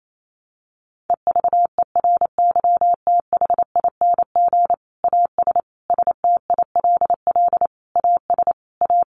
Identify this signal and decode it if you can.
Morse code, heard as a keyed tone, about 28 wpm, 700 Hz. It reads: EVERYTHING AS STILL AS A
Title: Sentences from Top 700 Words 28wpm